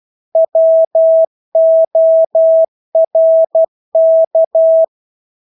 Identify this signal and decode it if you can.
Morse code, heard as a keyed tone, about 12 wpm, 650 Hz. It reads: WORK
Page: Engelske ord 12 wpm